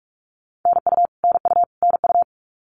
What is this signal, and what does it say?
Morse code, heard as a keyed tone, about 44 wpm, 700 Hz. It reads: NU NU NU
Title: Canadian Provinces and Territories - Repeated 3x - 45wpm